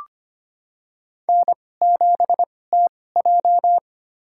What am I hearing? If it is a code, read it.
Morse code, heard as a keyed tone, about 25 wpm, spaced as for 18 wpm, 700 Hz. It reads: N7TJ
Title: USA Call Signs - 4 Characters - 2x Character Spacing - 25wpm